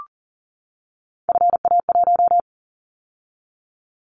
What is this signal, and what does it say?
Morse code, heard as a keyed tone, about 40 wpm, 700 Hz. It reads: FA1
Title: Callsign Prefixes for Top 50 DXCC - 40wpm